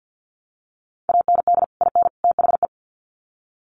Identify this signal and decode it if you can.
Morse code, heard as a keyed tone, about 50 wpm, 700 Hz. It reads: AND IN THE
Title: Sets of 3 Words 50wpm